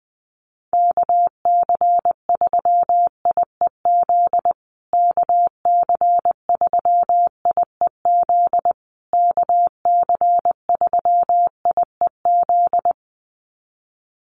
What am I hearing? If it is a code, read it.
Morse code, heard as a keyed tone, about 20 wpm, 700 Hz. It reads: KC3IEZ KC3IEZ KC3IEZ
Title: USA Call Signs Repeated 3x 20wpm